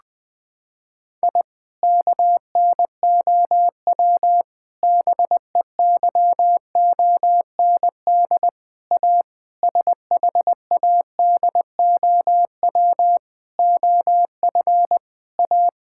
Morse code, heard as a keyed tone, about 20 wpm, 700 Hz. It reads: I KNOW BEYOND A SHADOW OF A
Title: Sentences with 7-Word Phrases - 20wpm